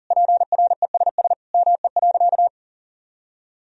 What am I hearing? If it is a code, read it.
Morse code, heard as a keyed tone, about 40 wpm, 700 Hz. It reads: PRESS ME.
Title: Sentences from Top 1000 Words - Review Part 1 - 40wpm